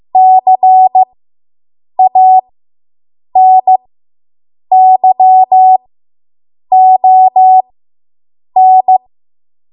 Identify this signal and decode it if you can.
Morse code, heard as a keyed tone, about 15 wpm, spaced as for 7 wpm, 750 Hz. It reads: CANYON